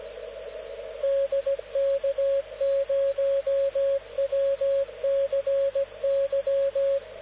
Morse code, heard as a keyed tone, about 17 wpm, 550 Hz, with radio noise. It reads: DK0WCY
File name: luxorion / Radio / dk0wcy-spaceweather-10144khz